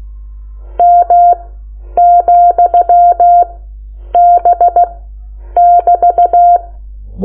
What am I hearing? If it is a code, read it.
Morse code, heard as a keyed tone, about 16 wpm, spaced as for 9 wpm, 700 Hz. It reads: M,B<BT>
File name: Lecciones / CLASE Nº9